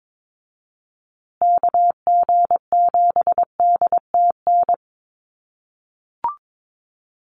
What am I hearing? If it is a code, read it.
Morse code, heard as a keyed tone, about 22 wpm, 700 Hz. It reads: KG7DTN
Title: USA Call Signs Repeated 2x 22wpm